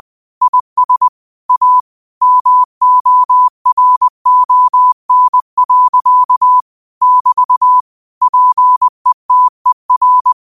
Morse code, heard as a keyed tone, about 20 wpm, 1000 Hz. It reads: IS A MORON. <BT> PETER
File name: qotdfiles / 20wpm